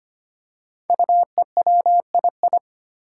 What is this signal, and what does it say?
Morse code, heard as a keyed tone, about 25 wpm, 700 Hz. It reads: UEWII
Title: Five Challenging Letter-Number Combinations 25wpm